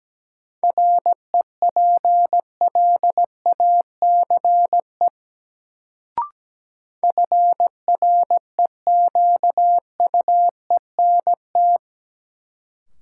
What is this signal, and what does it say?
Morse code, heard as a keyed tone, about 17 wpm, 700 Hz. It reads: REPLACE FREQUENT
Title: Sentences from Top 1400 Words - New Word Drill Alt - 17wpm